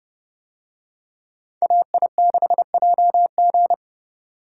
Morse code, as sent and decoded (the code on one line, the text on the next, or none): .- .. -.... .--- --.
AI6JG